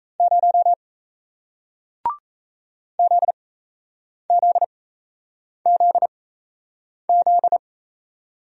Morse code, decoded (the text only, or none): 0 ZZZZ